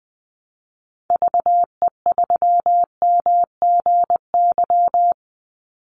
Morse code, heard as a keyed tone, about 20 wpm, 700 Hz. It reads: VE3MGY
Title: Call Signs Repeated 3x - 20wpm